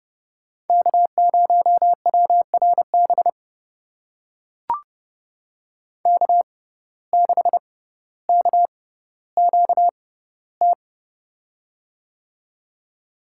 Morse code, heard as a keyed tone, about 30 wpm, spaced as for 15 wpm, 700 Hz. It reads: K0WRB K6KQT
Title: USA Call Signs - 5 Characters - 6x Character Spacing - 30wpm